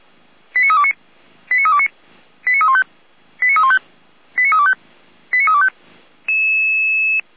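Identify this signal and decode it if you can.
Five-tone selective call (ZVEI), heard as quick runs of five tones, tones of about 70 ms. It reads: 89329 89329 89315 89315 89325 89325
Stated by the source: ZVEI five-tone selective call 89320 received via SDR.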